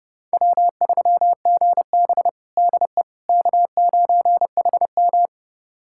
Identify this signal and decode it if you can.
Morse code, heard as a keyed tone, about 32 wpm, 700 Hz. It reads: W3GB DE K9HM
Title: USA Callsign DE Callsign - 4 Characters - 30wpm